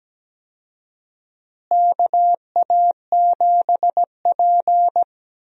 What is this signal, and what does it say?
Morse code, heard as a keyed tone, about 17 wpm, 700 Hz. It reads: KA7P
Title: USA Call Signs - 4 Characters - Repeated 3x - 17wpm